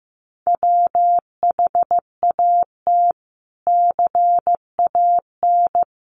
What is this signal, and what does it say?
Morse code, heard as a keyed tone, about 15 wpm, 700 Hz. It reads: WHAT CAN
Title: Sets of 5 Words Encore 15wpm